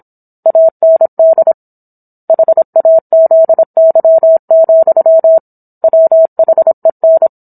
Morse code, heard as a keyed tone, about 27 wpm, 650 Hz. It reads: AND HAZY, WHEN